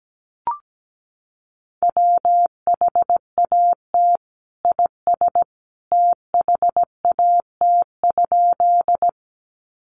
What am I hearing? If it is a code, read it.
Morse code, heard as a keyed tone, about 17 wpm, 700 Hz. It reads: WHAT IS THAT?
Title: Sentences from Top 100 Words 17wpm